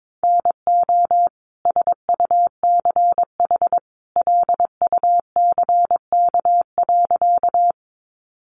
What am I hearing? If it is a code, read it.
Morse code, heard as a keyed tone, about 22 wpm, 700 Hz. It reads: NO SUCH LUCK.
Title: Sentences from Top 1500 Words 22wpm